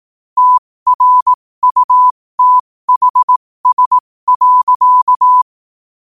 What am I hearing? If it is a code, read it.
Morse code, heard as a keyed tone, about 18 wpm, spaced as for 15 wpm, 1000 Hz. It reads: TRUTHS.
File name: qotdfiles / 15wpm